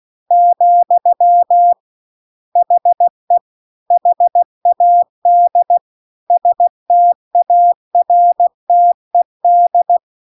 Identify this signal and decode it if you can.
Morse code, heard as a keyed tone, about 16 wpm, 700 Hz. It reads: , HE HAD STARTED